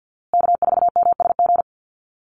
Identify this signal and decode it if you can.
Morse code, heard as a keyed tone, about 50 wpm, 700 Hz. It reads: K4MSG
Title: Call Signs Repeated 2x - 50wpm